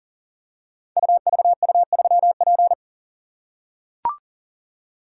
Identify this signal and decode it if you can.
Morse code, heard as a keyed tone, about 39 wpm, 700 Hz. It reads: UVU3P
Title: Code Groups - 5 Letter:Numbers - 40wpm